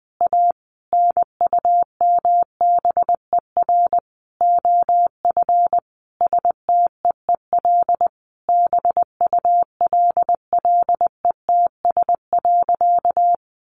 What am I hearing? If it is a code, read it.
Morse code, heard as a keyed tone, about 20 wpm, 700 Hz. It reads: A NUMBER OF STEEL BULLETS.
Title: A Sound Of Thunder 20wpm